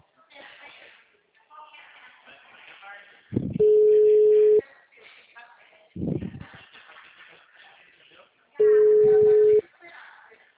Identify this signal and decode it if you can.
Ringback tone (Europe, 425 Hz), with speech.